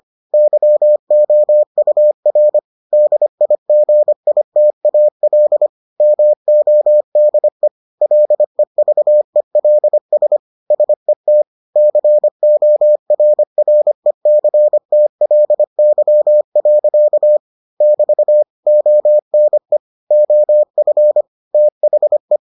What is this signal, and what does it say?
Morse code, heard as a keyed tone, about 25 wpm, 600 Hz. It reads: YOUR DIGITAL MODE LEVELS SET CORRECTLY. <BT> ONE OF THE